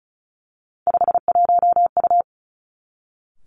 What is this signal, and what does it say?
Morse code, heard as a keyed tone, about 36 wpm, 700 Hz. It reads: 51U
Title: Three Challenging Letter-Number Combinations 35wpm